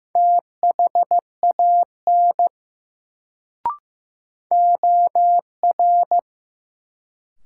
Morse code, heard as a keyed tone, about 15 wpm, 700 Hz. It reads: THAN OR
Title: Sentences from Top 100 Words - New Word Drill - 15wpm